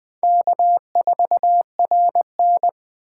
Morse code, heard as a keyed tone, about 20 wpm, 700 Hz. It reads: K4RN
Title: Speed Racing - USA 1x2 Call Signs 17wpm